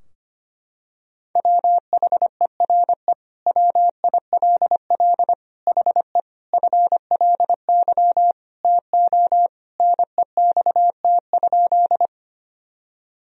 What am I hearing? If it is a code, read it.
Morse code, heard as a keyed tone, about 25 wpm, 700 Hz. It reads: WHERE WILL HE FLY TO NEXT?
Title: Sentences from Top 1000 Words - Review Part 2 - 25wpm